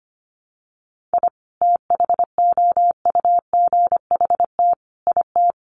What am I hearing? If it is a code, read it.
Morse code, heard as a keyed tone, about 25 wpm, 700 Hz. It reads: I THOUGHT IT
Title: Sets of 5 Words Encore 25wpm